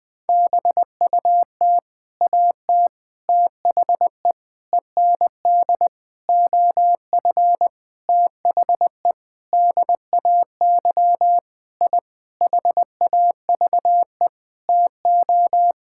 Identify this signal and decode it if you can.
Morse code, heard as a keyed tone, about 20 wpm, 700 Hz. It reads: BUT AT THE END OF THE DAY I HAVE TO
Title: Sentences with 8-Word Phrases - 20wpm